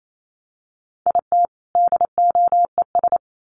Morse code, heard as a keyed tone, about 28 wpm, 700 Hz. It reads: IT DOES
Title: Sets of 2 Words 28wpm